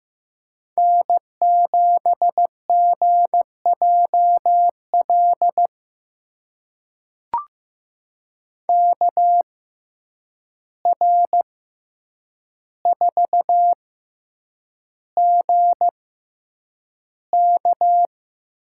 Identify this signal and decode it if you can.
Morse code, heard as a keyed tone, about 15 wpm, spaced as for 8 wpm, 700 Hz. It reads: N7GJL KR4GK
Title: USA Call Signs - 5 Characters - 6x Character Spacing - 15wpm